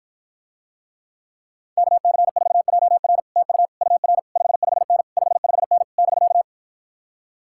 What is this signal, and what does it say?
Morse code, heard as a keyed tone, about 53 wpm, 700 Hz. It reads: KC4JR TU UR 55N 55N <BK>